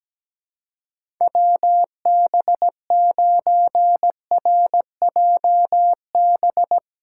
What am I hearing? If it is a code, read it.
Morse code, heard as a keyed tone, about 17 wpm, 700 Hz. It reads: WB9RJB
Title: USA Callsign DE Callsign 17wpm